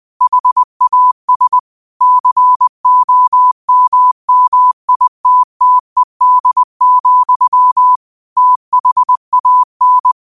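Morse code, heard as a keyed tone, about 20 wpm, 1000 Hz. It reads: HAS COMMITTED, THAN
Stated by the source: Morse code quote of the day at 20 wpm.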